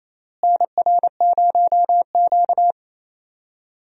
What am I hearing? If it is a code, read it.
Morse code, heard as a keyed tone, about 28 wpm, 700 Hz. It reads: NR0Q
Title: USA Call Signs - 4 Characters - Repeated 2x - 28wpm